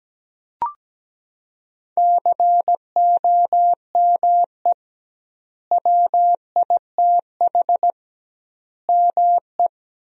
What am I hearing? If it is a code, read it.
Morse code, heard as a keyed tone, about 17 wpm, 700 Hz. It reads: COME WITH ME
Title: 2x Word-Spacing - Sentences from Top 200 Words - 17wpm